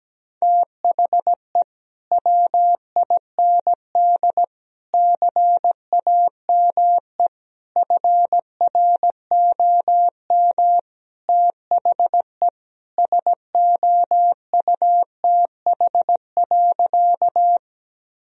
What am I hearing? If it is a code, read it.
Morse code, heard as a keyed tone, about 17 wpm, 700 Hz. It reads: THE WIND CAME FROM THE SOUTH.
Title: Sentences from Top 1000 Words - Review Part 2 - 17wpm